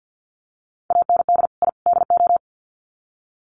Morse code, heard as a keyed tone, about 48 wpm, 700 Hz. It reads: AND I DO
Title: Sets of 3 Words 50wpm